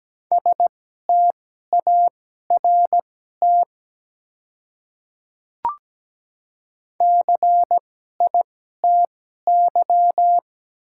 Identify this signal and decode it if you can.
Morse code, heard as a keyed tone, about 17 wpm, spaced as for 12 wpm, 700 Hz. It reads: START CITY
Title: 2x Character Spacing - Top 300 Words - 17wpm